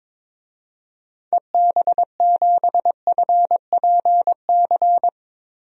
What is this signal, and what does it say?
Morse code, heard as a keyed tone, about 22 wpm, 700 Hz. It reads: EB7FPC
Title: DXCC-100 Call Signs - 6 Characters - Repeated 3x - 22wpm